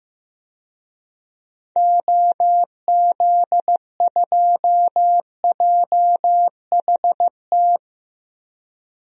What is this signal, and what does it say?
Morse code, heard as a keyed tone, about 15 wpm, 700 Hz. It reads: OZ2JHT
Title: DXCC-100 Call Signs - 6 Characters - Repeated 2x - 15wpm